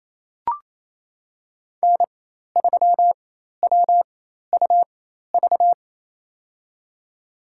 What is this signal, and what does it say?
Morse code, heard as a keyed tone, about 28 wpm, spaced as for 13 wpm, 690 Hz. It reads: N3WUV
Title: USA Call Signs - 5 Characters - 4x Character Spacing - 28wpm